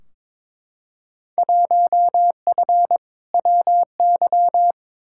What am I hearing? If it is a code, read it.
Morse code, heard as a keyed tone, about 22 wpm, 700 Hz. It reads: 1F WY
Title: Field Day Exchanges Encore - 22wpm